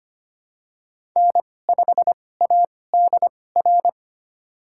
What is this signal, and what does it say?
Morse code, heard as a keyed tone, about 25 wpm, spaced as for 18 wpm, 700 Hz. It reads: N5ADR
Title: USA Call Signs - 5 Characters - 2x Character Spacing - 25wpm